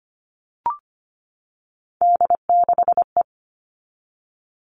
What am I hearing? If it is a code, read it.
Morse code, heard as a keyed tone, about 25 wpm, 700 Hz. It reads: D6E
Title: All Cut Numbers - 3-Digit 25wpm